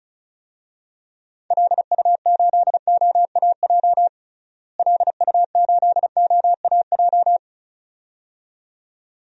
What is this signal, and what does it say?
Morse code, heard as a keyed tone, about 36 wpm, 700 Hz. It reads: LU8OAJ LU8OAJ
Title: DXCC-100 Call Signs - 6 Characters - Repeated 2x - 35wpm